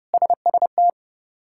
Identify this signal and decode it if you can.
Morse code, heard as a keyed tone, about 30 wpm, 700 Hz. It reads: SST
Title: K1USN SST QSO Practice - 30wpm